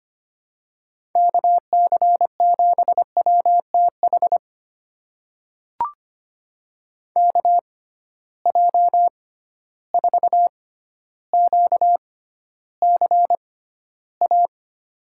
Morse code, heard as a keyed tone, about 25 wpm, spaced as for 13 wpm, 700 Hz. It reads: KC7WTH KJ4QCA
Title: USA Call Signs - 6 Characters - 6x Character Spacing - 25wpm